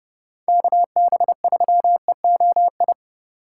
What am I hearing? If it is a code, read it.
Morse code, heard as a keyed tone, about 30 wpm, 700 Hz. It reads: KB3EOI
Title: USA Call Signs - 6 Characters - 2x Character Spacing - 30wpm